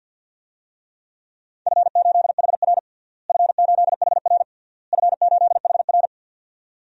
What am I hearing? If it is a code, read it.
Morse code, heard as a keyed tone, about 52 wpm, 700 Hz. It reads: F8HR F8HR F8HR
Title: DXCC-100 Call Signs - 4 Characters - Repeated 3x - 50wpm